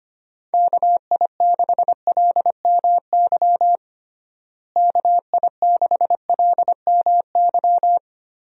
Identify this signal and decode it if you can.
Morse code, heard as a keyed tone, about 25 wpm, 700 Hz. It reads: KI6LMY KI6LMY
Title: USA Call Signs - 6 Characters - Repeated 3x - 25wpm